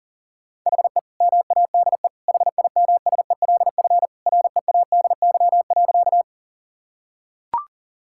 Morse code, heard as a keyed tone, about 37 wpm, 700 Hz. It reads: HE MADE HIMSELF READY.
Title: Sentences from Top 1600 Words 40wpm